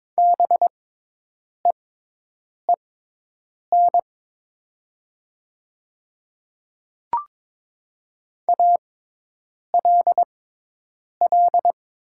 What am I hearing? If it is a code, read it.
Morse code, heard as a keyed tone, about 22 wpm, spaced as for 8 wpm, 700 Hz. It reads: BEEN ALL
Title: 6x Character Spacing - Top 100 Words - 22wpm